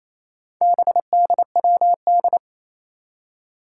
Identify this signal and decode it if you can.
Morse code, heard as a keyed tone, about 28 wpm, 700 Hz. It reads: BDWD
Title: Four Challenging Letter-Number Combinations 28wpm